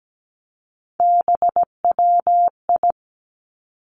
Morse code, heard as a keyed tone, about 17 wpm, 700 Hz. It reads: BWI
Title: Three Challenging Letter-Number Combinations 17wpm